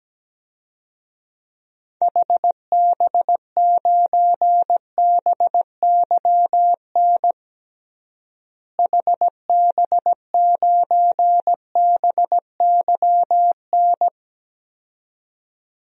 Morse code, heard as a keyed tone, about 17 wpm, 700 Hz. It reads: HB9BYN HB9BYN
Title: DXCC-100 Call Signs - 6 Characters - Repeated 2x - 17wpm